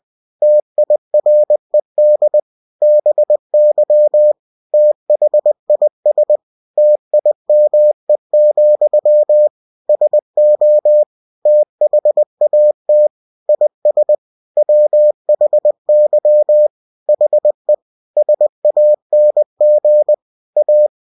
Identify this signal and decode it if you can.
Morse code, heard as a keyed tone, about 20 wpm, 600 Hz. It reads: TIRED BY THIS TIME, SO THAT IS WHY HE SANG A
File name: cw-books / winnie-the-pooh / winnie-20wpm-page-0003